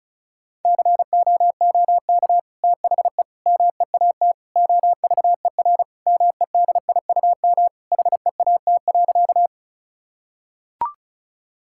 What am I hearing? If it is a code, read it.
Morse code, heard as a keyed tone, about 36 wpm, 700 Hz. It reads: COOK THE MEAT OVER MEDIUM HEAT.
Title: Sentences from Top 1000 Words - Review Part 6 - 35wpm